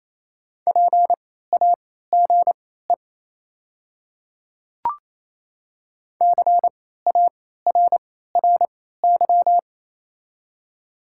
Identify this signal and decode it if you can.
Morse code, heard as a keyed tone, about 28 wpm, spaced as for 16 wpm, 700 Hz. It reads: PAGE CARRY
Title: 3x Character Spacing - Top 300 Words - 28wpm